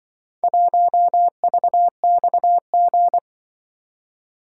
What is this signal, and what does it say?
Morse code, heard as a keyed tone, about 24 wpm, 700 Hz. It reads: 1VXG
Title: Speed-Racing - Four Letter-Number Combinations - 20wpm